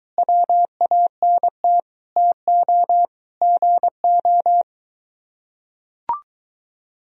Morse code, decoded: WANT TO GO

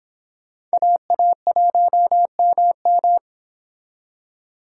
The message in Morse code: AA1MM

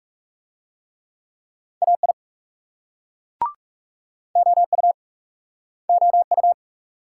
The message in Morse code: AI OU OU

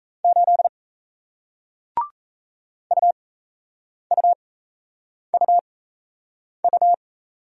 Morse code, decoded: 8 UUUU

